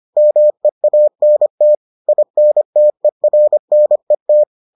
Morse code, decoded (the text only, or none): MEANT INTERNET